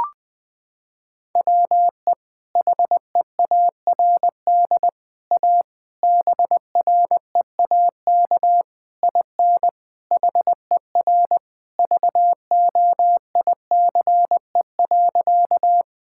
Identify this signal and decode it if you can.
Morse code, heard as a keyed tone, about 20 wpm, 700 Hz. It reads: WE HEARD A BREAK IN HER VOICE.
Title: Sentences from Top 1000 Words - Review Part 6 - 20wpm